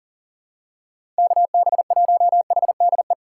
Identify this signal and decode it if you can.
Morse code, heard as a keyed tone, about 40 wpm, 700 Hz. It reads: KB1HDE
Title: USA Call Signs - 4x Character Spacing - 40wpm